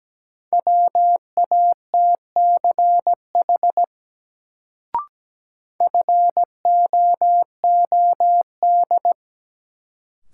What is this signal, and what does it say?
Morse code, heard as a keyed tone, about 17 wpm, 700 Hz. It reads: WATCH FOOD
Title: Sentences from Top 300 Words - New Word Drill - 17wpm